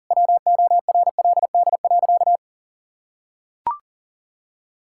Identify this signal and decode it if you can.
Morse code, heard as a keyed tone, about 40 wpm, 700 Hz. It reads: WORLD.
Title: Sentences from Top 1000 Words - Review Part 6 - 40wpm